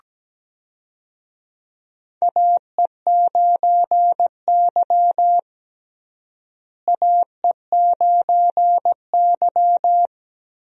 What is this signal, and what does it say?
Morse code, heard as a keyed tone, about 17 wpm, 700 Hz. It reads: AE9Y AE9Y
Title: USA Call Signs - 4 Characters - Repeated 2x - 17wpm